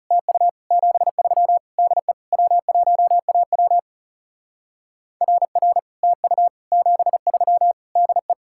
Morse code, heard as a keyed tone, about 36 wpm, 700 Hz. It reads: TU 73 DE W1AW RR TU 73 DE
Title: Speed-Racing Common QSO Phrases - 30wpm